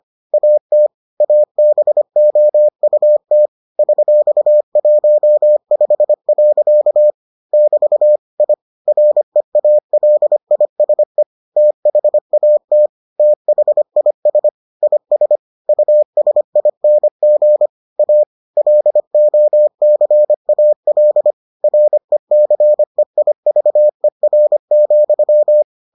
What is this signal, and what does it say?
Morse code, heard as a keyed tone, about 25 wpm, 600 Hz. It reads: AT ABOUT $15. <BT> I REALISE THAT THIS IS USING A LOCAL RECEIVER,